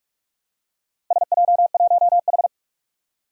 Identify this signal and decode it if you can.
Morse code, heard as a keyed tone, about 44 wpm, 700 Hz. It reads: SJ1H
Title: Four Challenging Letter-Number Combinations 45wpm